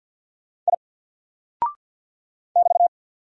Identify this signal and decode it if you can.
Morse code, heard as a keyed tone, about 49 wpm, 700 Hz. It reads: I <BT>
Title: Single Character (Letter:Number:Punctuation:Prosign) 50wpm